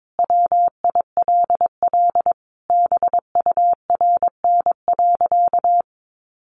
Morse code, heard as a keyed tone, about 22 wpm, 700 Hz. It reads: WILL BURN.
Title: Sentences from Top 1300 Words 22wpm